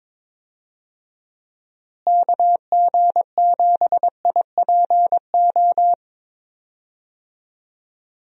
Morse code, heard as a keyed tone, about 21 wpm, 700 Hz. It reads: KG7IPO
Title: USA Call Signs - 6 Characters - Repeated 2x - 22wpm